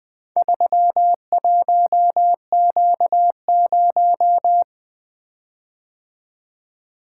Morse code, heard as a keyed tone, about 20 wpm, 700 Hz. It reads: 31Q0
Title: Four Letter-Number Combinations - 20wpm